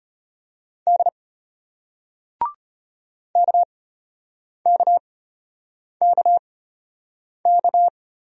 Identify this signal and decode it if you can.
Morse code, heard as a keyed tone, about 35 wpm, spaced as for 8 wpm, 700 Hz. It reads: D KKKK